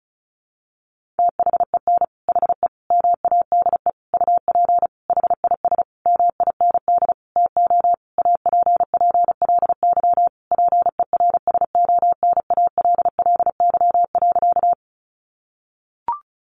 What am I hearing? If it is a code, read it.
Morse code, heard as a keyed tone, about 34 wpm, 700 Hz. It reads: THEN HE MADE UP HIS MIND TO APPLY PERSONALLY.
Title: Sentences from Top 1400 Words 35wpm